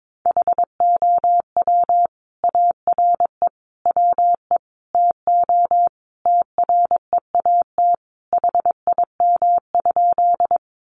Morse code, heard as a keyed tone, about 22 wpm, 700 Hz. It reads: HOW ARE WE TO TREAT HIM?
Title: Sentences from Top 1300 Words 22wpm